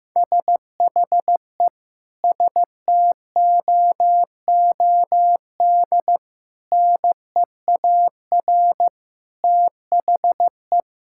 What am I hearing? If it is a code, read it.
Morse code, heard as a keyed tone, about 16 wpm, 700 Hz. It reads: SHE STOOD NEAR THE
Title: Sentences from Top 1600 Words 15wpm